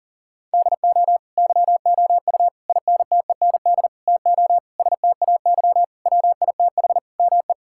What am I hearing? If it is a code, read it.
Morse code, heard as a keyed tone, about 40 wpm, 700 Hz. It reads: DO YOU INTEND TO STAY WITH ME
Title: Sentences from Top 1600 Words 40wpm